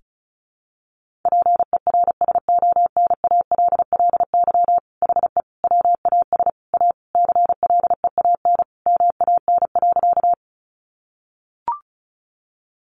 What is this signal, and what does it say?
Morse code, heard as a keyed tone, about 34 wpm, 700 Hz. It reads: PERSONALLY HE WAS A CLEAN MAN.
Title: Sentences from Top 1400 Words 35wpm